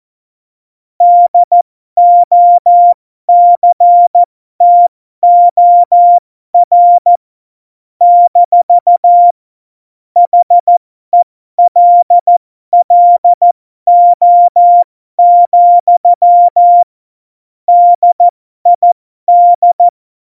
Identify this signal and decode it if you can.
Morse code, heard as a keyed tone, about 14 wpm, spaced as for 12 wpm, 700 Hz. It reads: DOCTOR - HELLO, DID
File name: Morsekurs / Witze